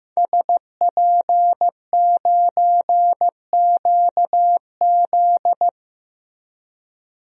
Morse code, heard as a keyed tone, about 15 wpm, 700 Hz. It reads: SP9QZ
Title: DXCC-100 Call Signs - 5 Characters - Repeated 3x - 15wpm